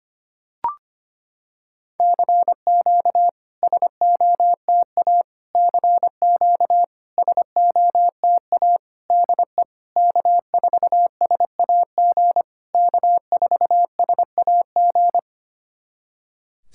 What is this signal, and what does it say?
Morse code, heard as a keyed tone, about 25 wpm, 700 Hz. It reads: CQ SOTA CQ SOTA DE K4SAG K4SAG
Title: SOTA QSO Practice 25wpm